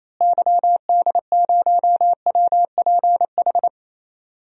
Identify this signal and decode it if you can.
Morse code, heard as a keyed tone, about 28 wpm, 700 Hz. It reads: YD0WPH
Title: DXCC-100 Call Signs - 6 Characters - Repeated 3x - 28wpm